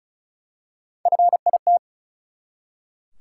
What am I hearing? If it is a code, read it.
Morse code, heard as a keyed tone, about 34 wpm, 700 Hz. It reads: FIT
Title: Three Letter Words - 35wpm